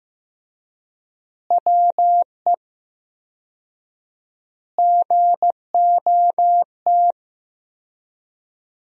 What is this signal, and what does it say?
Morse code, heard as a keyed tone, about 15 wpm, 700 Hz. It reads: WE GOT
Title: 4x Word-Spacing - Sentences from Top 500 Words - 15wpm